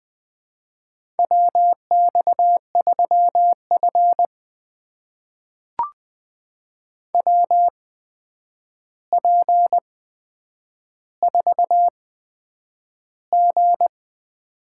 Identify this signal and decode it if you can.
Morse code, heard as a keyed tone, about 20 wpm, spaced as for 9 wpm, 700 Hz. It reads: WX3F WP4G
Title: USA Call Signs - 4 Characters - 8x Character Spacing - 20wpm